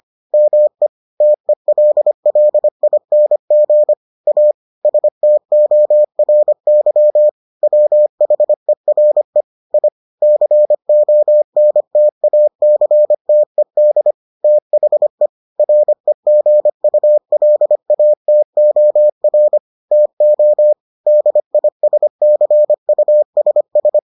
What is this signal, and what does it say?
Morse code, heard as a keyed tone, about 25 wpm, 600 Hz. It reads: ME TELLING A STORY WHERE I CONTACTED THE REGULATOR TO DISCUSS